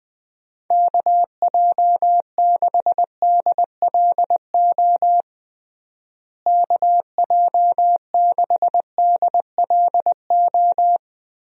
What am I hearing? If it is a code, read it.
Morse code, heard as a keyed tone, about 20 wpm, 700 Hz. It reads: KJ6DLO KJ6DLO
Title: USA Call Signs - 6 Characters - Repeated 3x - 20wpm